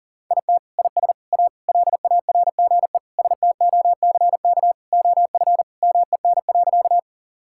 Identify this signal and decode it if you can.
Morse code, heard as a keyed tone, about 40 wpm, 700 Hz. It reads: IT IS A LARGE STOCK OF MEN.
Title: Sentences from Top 1000 Words - Review Part 5 - 40wpm